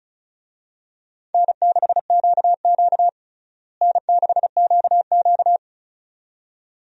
Morse code, heard as a keyed tone, about 34 wpm, 700 Hz. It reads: N6QQ N6QQ
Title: USA Call Signs - 4 Characters - Repeated 2x - 35wpm